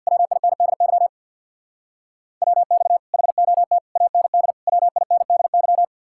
Morse code, heard as a keyed tone, about 67 wpm, 690 Hz. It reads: WINDY WX HOT AND WINDY